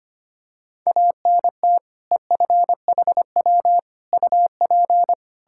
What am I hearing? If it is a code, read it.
Morse code, heard as a keyed tone, about 25 wpm, 700 Hz. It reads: ANT EFHW UP
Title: K1USN SST QSO Practice 25wpm